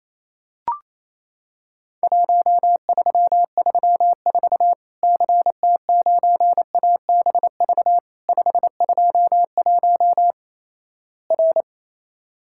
Morse code, heard as a keyed tone, about 28 wpm, 700 Hz. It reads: 1334 CT9ABV 521 R
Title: Worked All Europe DX Contest - QTC Exchange - 28wpm